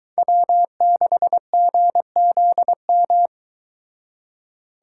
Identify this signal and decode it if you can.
Morse code, heard as a keyed tone, about 23 wpm, 700 Hz. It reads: W6GZM